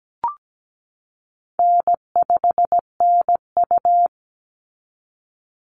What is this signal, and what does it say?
Morse code, heard as a keyed tone, about 17 wpm, 700 Hz. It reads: N5NU
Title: Easy Call Signs Repeated 2x - 17wpm